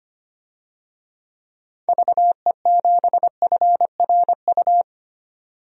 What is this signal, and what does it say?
Morse code, heard as a keyed tone, about 25 wpm, 700 Hz. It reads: VE7FRU
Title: DXCC-100 Call Signs - 6 Characters - 25wpm